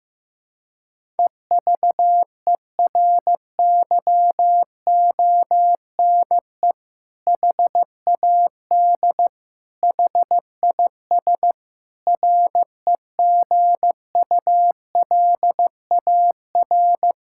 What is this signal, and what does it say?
Morse code, heard as a keyed tone, about 16 wpm, 700 Hz. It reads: EVERYONE HAD HIS REGULAR
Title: Sentences from Top 1000 Words - Review Part 6 - 15wpm